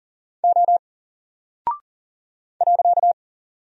Morse code, heard as a keyed tone, about 39 wpm, 700 Hz. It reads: O .
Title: Single Character (Letter:Number:Punctuation:Prosign) Alt 40wpm